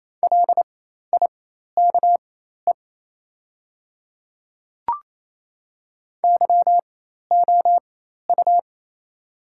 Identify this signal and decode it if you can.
Morse code, heard as a keyed tone, about 28 wpm, spaced as for 13 wpm, 700 Hz. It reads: LIKE YOU
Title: 4x Character Spacing - Top 100 Words - 28wpm